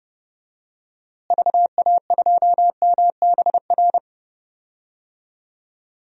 Morse code, heard as a keyed tone, about 30 wpm, 700 Hz. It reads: VA2MBR
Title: DXCC-100 Call Signs - 6 Characters - Repeated 3x - 30wpm